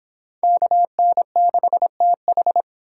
Morse code, heard as a keyed tone, about 24 wpm, 700 Hz. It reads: KN6TH